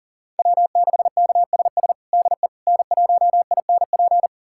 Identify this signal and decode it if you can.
Morse code, heard as a keyed tone, about 40 wpm, 700 Hz. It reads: W6KSS DE N1INP
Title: USA Callsign DE Callsign - 5 Characters - 40wpm